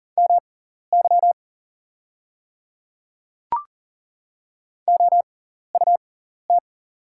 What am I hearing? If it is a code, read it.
Morse code, heard as a keyed tone, about 40 wpm, spaced as for 14 wpm, 700 Hz. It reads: MY OUT
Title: 6x Character Spacing - Top 100 Words - 40wpm